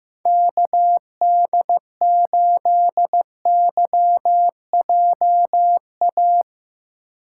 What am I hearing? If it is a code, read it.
Morse code, heard as a keyed tone, about 15 wpm, 700 Hz. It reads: KD8YJA
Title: USA Call Signs - 6 Characters - 4x Character Spacing - 15wpm